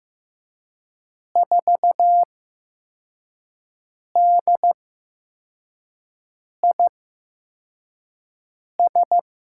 Morse code, heard as a keyed tone, about 15 wpm, spaced as for 4 wpm, 700 Hz. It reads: 4DIS
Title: USA Call Signs - 8x Character Spacing - 15wpm